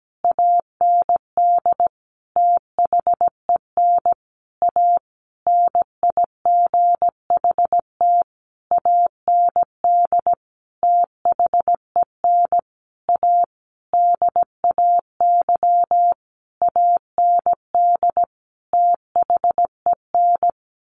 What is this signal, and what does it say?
Morse code, heard as a keyed tone, about 17 wpm, 700 Hz. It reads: AND THEN A NIGHT AND THEN A DAY AND THEN
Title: A Sound Of Thunder 17wpm